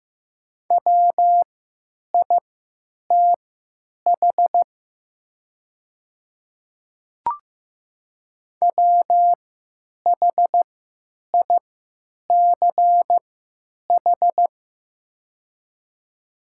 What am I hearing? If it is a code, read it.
Morse code, heard as a keyed tone, about 15 wpm, spaced as for 9 wpm, 700 Hz. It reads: WITH WHICH
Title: 3x Character Spacing - Top 100 Words - 15wpm